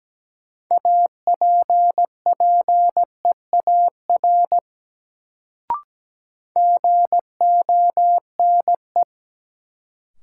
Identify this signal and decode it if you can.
Morse code, heard as a keyed tone, about 17 wpm, 700 Hz. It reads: APPEAR GONE
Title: Sentences from Top 1200 Words - New Word Drill Alt - 17wpm